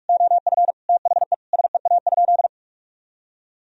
Morse code, heard as a keyed tone, about 41 wpm, 700 Hz. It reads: OF THE SEA?
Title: Sentences from Top 1000 Words - Review Part 3 - 45wpm